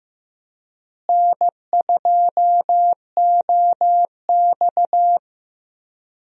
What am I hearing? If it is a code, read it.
Morse code, heard as a keyed tone, about 16 wpm, 700 Hz. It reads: N2OX
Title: USA Call Signs - 4 Characters - Repeated 3x - 15wpm